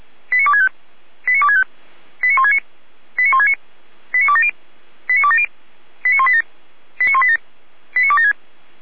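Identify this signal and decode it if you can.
Five-tone selective call (ZVEI), heard as quick runs of five tones, tones of about 70 ms. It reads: 89265 89265 89169 89169 89270 89270 89187 89187 89276
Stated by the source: ZVEI five-tone selective call received on SDR.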